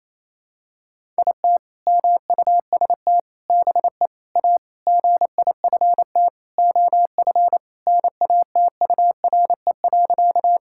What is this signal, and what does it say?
Morse code, heard as a keyed tone, about 28 wpm, 700 Hz. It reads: IT MUST BE A GIFT OF NATURE.
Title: Sentences from Top 1000 Words - Review Part 6 - 28wpm